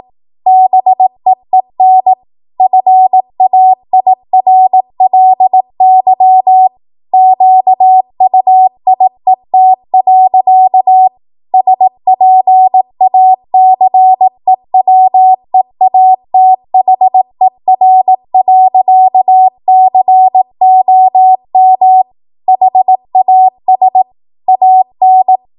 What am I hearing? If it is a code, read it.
Morse code, heard as a keyed tone, about 18 wpm, 750 Hz. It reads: BEEN FAIRLY QUIET. SPACEWEATHER.COM HAS AN